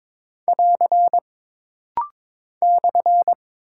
Morse code, heard as a keyed tone, about 22 wpm, 700 Hz. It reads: <AR> /
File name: practice / 22wpm / Single Punctuation-Prosign - Rapid-Fire - 22wpm